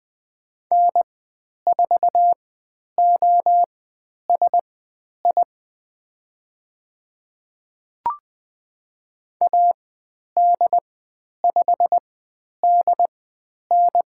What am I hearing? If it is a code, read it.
Morse code, heard as a keyed tone, about 20 wpm, spaced as for 10 wpm, 700 Hz. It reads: N4OSI AD5DN
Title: 095 Lesson 16g - Letter c - Call Signs 20wpm